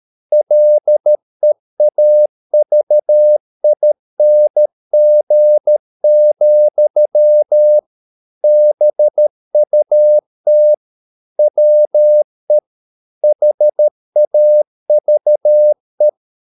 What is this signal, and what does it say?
Morse code, heard as a keyed tone, about 13 wpm, 600 Hz. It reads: LEAVING, BUT WE HAVE